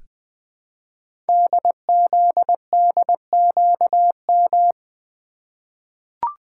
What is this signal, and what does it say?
Morse code, heard as a keyed tone, about 20 wpm, 700 Hz. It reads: DZDQM
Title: Code Groups - 5 Letters - 20wpm